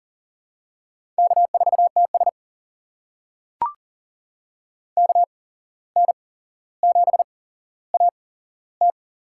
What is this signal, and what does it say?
Morse code, heard as a keyed tone, about 40 wpm, spaced as for 16 wpm, 700 Hz. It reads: K4TS KN7AT
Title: USA Call Signs - 8x Character Spacing - 40wpm